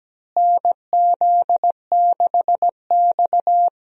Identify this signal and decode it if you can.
Morse code, heard as a keyed tone, about 17 wpm, 700 Hz. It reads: NZ6X
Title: USA Call Signs - 4 Characters - Repeated 2x - 17wpm